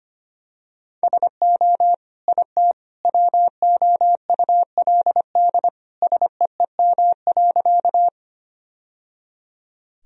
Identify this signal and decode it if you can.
Morse code, heard as a keyed tone, about 25 wpm, 700 Hz. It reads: SO IT WOULD SEEM.
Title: Sentences from Top 1000 Words - Review Part 1 - 25wpm